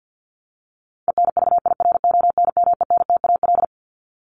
Morse code, heard as a keyed tone, about 50 wpm, 700 Hz. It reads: ENVIRONMENTAL
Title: Top 1001-1250 Words 50wpm